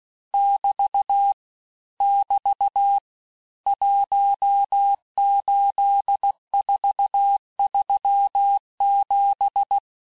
Morse code, heard as a keyed tone, about 16 wpm, 800 Hz. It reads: <BT> <BT> 18437
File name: cours-du-lundi files / cours-1592--teste-chiffre-vitesse-15mpm